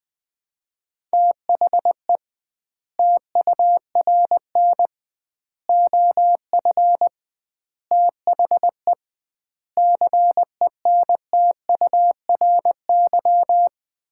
Morse code, heard as a keyed tone, about 20 wpm, 700 Hz. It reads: THE TURN OF THE CENTURY
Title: 2x Word-Spacing - Sets of 5 Words - 20wpm